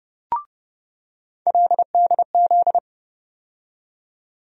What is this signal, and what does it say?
Morse code, heard as a keyed tone, about 30 wpm, 700 Hz. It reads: LDZ
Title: USA Call Sign Postfixes - 3 Letters 30wpm